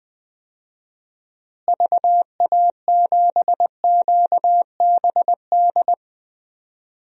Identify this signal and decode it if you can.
Morse code, heard as a keyed tone, about 20 wpm, 700 Hz. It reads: VA7QBD
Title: DXCC-100 Call Signs - 6 Characters - Repeated 3x - 20wpm